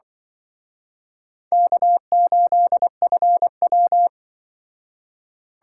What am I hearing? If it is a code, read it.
Morse code, heard as a keyed tone, about 24 wpm, 700 Hz. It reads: K8FW
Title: Speed Racing - USA 1x2 Call Signs 20wpm